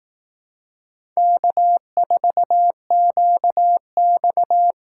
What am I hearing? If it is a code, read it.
Morse code, heard as a keyed tone, about 18 wpm, 700 Hz. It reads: K4QX
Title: Speed Racing - USA 1x2 Call Signs 15wpm